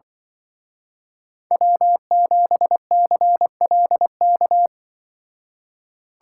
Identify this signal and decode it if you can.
Morse code, heard as a keyed tone, about 23 wpm, 700 Hz. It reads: W7CLK
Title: Speed Racing - USA 1x3 Call Signs 20wpm